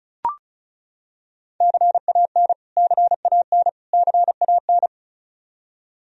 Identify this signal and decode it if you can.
Morse code, heard as a keyed tone, about 34 wpm, 700 Hz. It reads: CAN CAN CAN
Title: Top 100 Words Repeated 35wpm